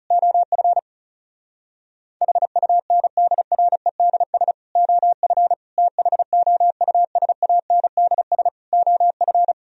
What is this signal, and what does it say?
Morse code, heard as a keyed tone, about 40 wpm, 700 Hz. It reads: OF HUNDREDS OF THOUSANDS OF